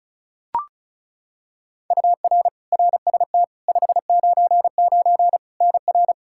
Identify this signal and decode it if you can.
Morse code, heard as a keyed tone, about 36 wpm, 700 Hz. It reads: UR RST 599 NR
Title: Practice QSO - Exchange 1 - Beginning - 35wpm